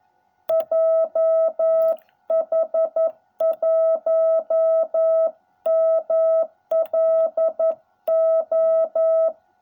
Morse code, heard as a keyed tone, about 11 wpm, 660 Hz. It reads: JH1MLO